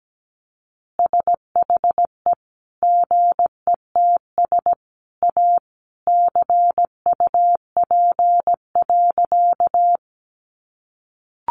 Morse code, heard as a keyed tone, about 17 wpm, 700 Hz. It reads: SHE GETS A CUP.
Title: Additional Easy Phrases 17wpm